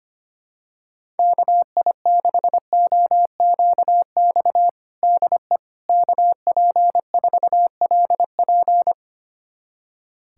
Morse code, heard as a keyed tone, about 25 wpm, 700 Hz. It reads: KI6OQX DE KP4LP
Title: USA Callsign DE Callsign 25wpm